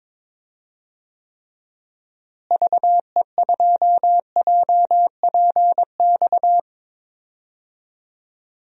Morse code, heard as a keyed tone, about 21 wpm, 700 Hz. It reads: VE2JPX
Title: DXCC-100 Call Signs - 6 Characters - Repeated 2x - 22wpm